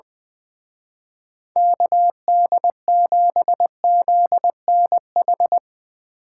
Morse code, heard as a keyed tone, about 23 wpm, 700 Hz. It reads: KD7ZNH
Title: Speed Racing - USA 2x3 Call Signs 20wpm